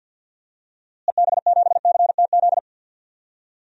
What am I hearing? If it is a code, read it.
Morse code, heard as a keyed tone, about 46 wpm, 700 Hz. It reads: EB7CTZ
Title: DXCC-100 Call Signs - 6 Characters - Repeated 3x - 50wpm